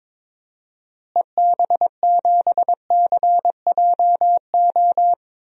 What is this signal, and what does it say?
Morse code, heard as a keyed tone, about 22 wpm, 700 Hz. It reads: EB7CJO
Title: DXCC-100 Call Signs - 6 Characters - Repeated 3x - 22wpm